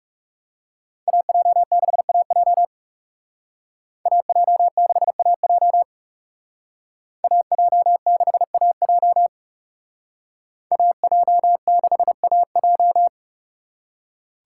Morse code, heard as a keyed tone, about 35 wpm, 700 Hz. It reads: AJ6AJ AJ6AJ AJ6AJ AJ6AJ
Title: Speed Racing - USA Call Signs 30wpm